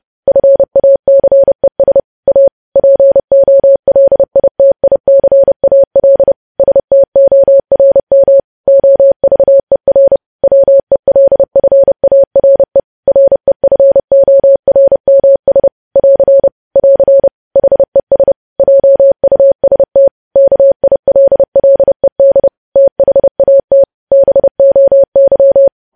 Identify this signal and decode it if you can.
Morse code, heard as a keyed tone, about 30 wpm, 550 Hz. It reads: FACES A POLITICAL STORM OVER WELFARE REFORMS <AR> <AR> HES JUST KILLED THAT BOY
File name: podcast / News headlines-1x-30x30